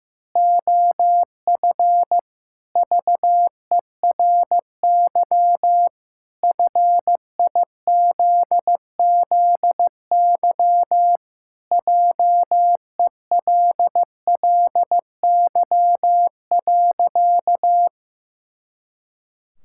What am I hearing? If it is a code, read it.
Morse code, heard as a keyed tone, about 15 wpm, 700 Hz. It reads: OF VERY FIZZY JELLY.
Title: Pangram Challenge v2 - 15wpm